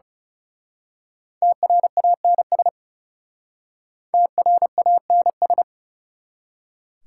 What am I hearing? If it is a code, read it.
Morse code, heard as a keyed tone, about 34 wpm, 690 Hz. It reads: TRANS TRANS
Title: Speed-Racing Prefixes and Suffixes 30wpm